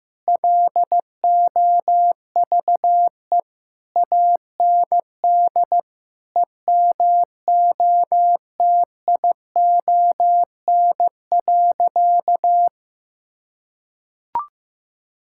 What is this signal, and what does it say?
Morse code, heard as a keyed tone, about 15 wpm, 700 Hz. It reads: LOVE AND EMOTION.
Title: Sentences from Top 1600 Words 15wpm